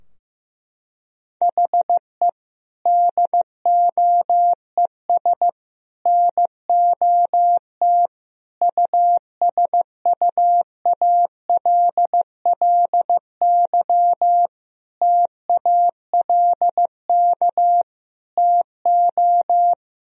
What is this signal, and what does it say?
Morse code, heard as a keyed tone, about 15 wpm, 700 Hz. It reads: HE DOES NOT USUALLY TALK TO
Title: Sentences from Top 1000 Words - Review Part 3 - 15wpm